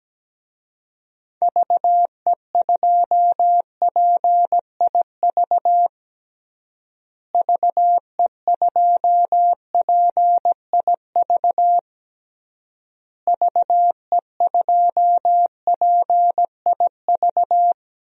Morse code, heard as a keyed tone, about 17 wpm, 700 Hz. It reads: VE2PIV VE2PIV VE2PIV
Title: DXCC-100 Call Signs - 6 Characters - Repeated 3x - 17wpm